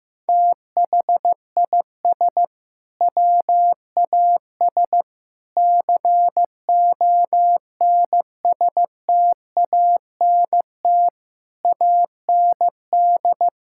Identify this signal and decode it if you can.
Morse code, heard as a keyed tone, about 15 wpm, 700 Hz. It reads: THIS WAS CONSTANT AND
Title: Sentences from Top 1600 Words 15wpm